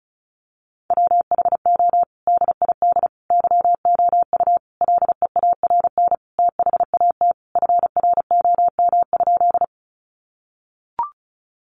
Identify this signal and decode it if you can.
Morse code, heard as a keyed tone, about 34 wpm, 700 Hz. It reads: WHO DID YOU LEARN THAT FROM?
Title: Sentences from Top 300 Words 35wpm